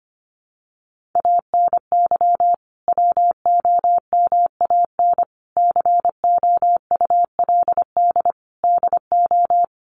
Morse code, heard as a keyed tone, about 25 wpm, 700 Hz. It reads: ANY WOMAN COULD DO
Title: Sentences from Top 1100 Words 25wpm